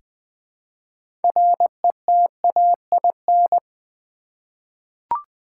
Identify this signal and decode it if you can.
Morse code, heard as a keyed tone, about 20 wpm, 700 Hz. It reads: RETAIN
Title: Six Letter Words - 20wpm